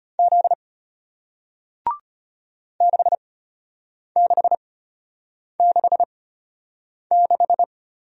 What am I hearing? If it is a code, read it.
Morse code, heard as a keyed tone, about 33 wpm, spaced as for 8 wpm, 700 Hz. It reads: Z 6666